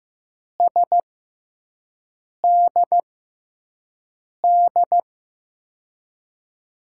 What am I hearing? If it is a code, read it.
Morse code, heard as a keyed tone, about 15 wpm, spaced as for 5 wpm, 700 Hz. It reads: SDD
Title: USA Call Signs - 6 Characters - 6x Character Spacing - 15wpm